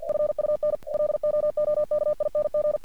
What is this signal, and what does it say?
Morse code, heard as a keyed tone, about 48 wpm, 600 Hz. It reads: FUN LOOKING
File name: archived / CRSnet